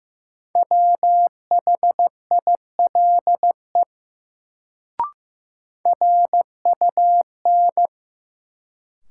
Morse code, heard as a keyed tone, about 15 wpm, 700 Hz. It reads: WHILE RUN
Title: Sentences from Top 300 Words - New Word Drill - 15wpm